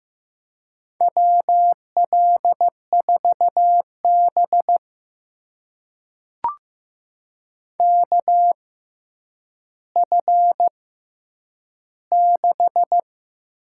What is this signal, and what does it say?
Morse code, heard as a keyed tone, about 15 wpm, spaced as for 9 wpm, 700 Hz. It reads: WL4B KF6
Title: USA Call Signs - 6x Character Spacing - 15wpm